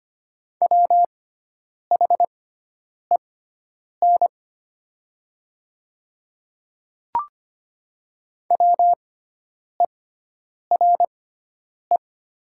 Morse code, heard as a keyed tone, about 25 wpm, spaced as for 9 wpm, 700 Hz. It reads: WHEN WERE
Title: 6x Character Spacing - Top 100 Words - 25wpm